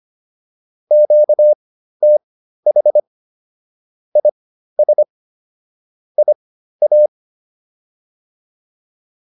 Morse code, heard as a keyed tone, about 25 wpm, spaced as for 13 wpm, 600 Hz. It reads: QTH IS IA